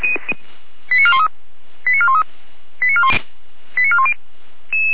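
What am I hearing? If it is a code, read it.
Five-tone selective call (ZVEI), heard as quick runs of five tones, tones of about 70 ms. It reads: 89413 89413 89410
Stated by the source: ZVEI five-tone selective call received by SDR, archived by Feuerwehr Orlamünde (fire brigade).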